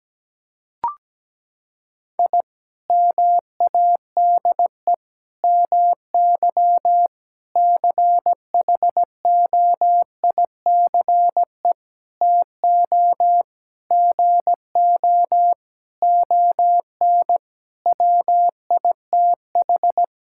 Morse code, heard as a keyed tone, about 17 wpm, 700 Hz. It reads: I MADE MY CHOICE TO GO ON WITH
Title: Sentences from Top 1000 Words - Review Part 6 - 17wpm